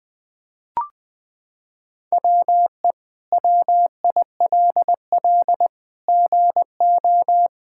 Morse code, heard as a keyed tone, about 20 wpm, 700 Hz. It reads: WE WILL GO
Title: Sentences from Top 1000 Words - Review Part 6 - 20wpm